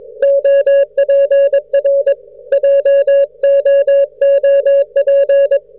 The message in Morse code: OPR JOOP